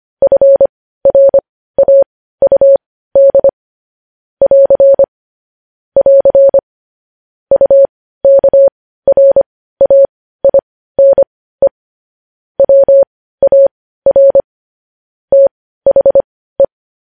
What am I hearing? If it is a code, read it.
Morse code, heard as a keyed tone, about 25 wpm, spaced as for 15 wpm, 550 Hz. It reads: FRAUD <AR> <AR> UKRAINE WAR THE